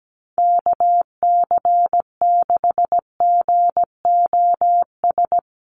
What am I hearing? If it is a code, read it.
Morse code, heard as a keyed tone, about 17 wpm, 700 Hz. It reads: KC6GOS
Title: USA Call Signs Repeated 3x 17wpm